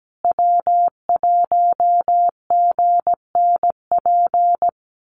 Morse code, heard as a keyed tone, about 17 wpm, 700 Hz. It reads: W1GNP
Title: USA Call Signs Repeated 3x 17wpm